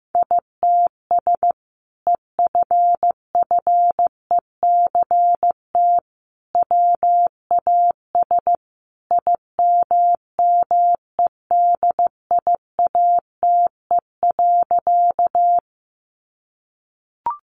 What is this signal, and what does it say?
Morse code, heard as a keyed tone, about 15 wpm, 700 Hz. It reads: ITS EFFECT WAS IMMEDIATE.
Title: Sentences from Top 1200 Words 15wpm